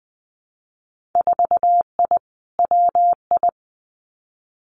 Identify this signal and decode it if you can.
Morse code, heard as a keyed tone, about 20 wpm, 700 Hz. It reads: 4I WI
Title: Winter Field Day Exchanges 20wpm